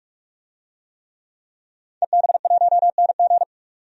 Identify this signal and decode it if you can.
Morse code, heard as a keyed tone, about 41 wpm, 700 Hz. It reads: EB1NG
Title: DXCC-100 Call Signs - 5 Characters - 45wpm